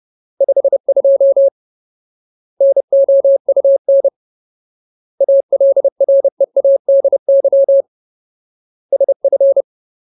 Morse code, heard as a keyed tone, about 30 wpm, 550 Hz. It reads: 52 NOUN ALREADY SF
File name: Ultimate Word List / 30WPM Ultimate